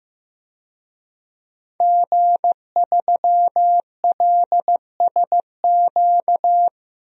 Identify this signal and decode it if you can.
Morse code, heard as a keyed tone, about 15 wpm, 700 Hz. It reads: G3LSQ
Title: DXCC-100 Call Signs - 5 Characters - 15wpm